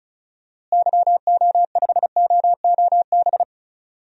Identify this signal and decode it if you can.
Morse code, heard as a keyed tone, about 36 wpm, 700 Hz. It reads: YO5OOB
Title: DXCC-100 Call Signs - 6 Characters - Repeated 3x - 35wpm